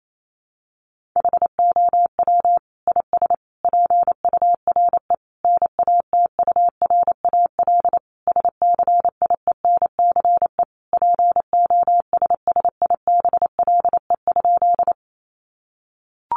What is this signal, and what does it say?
Morse code, heard as a keyed tone, about 28 wpm, 700 Hz. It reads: HOW IS PURE NATURAL SCIENCE POSSIBLE?
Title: Sentences from Top 1500 Words 28wpm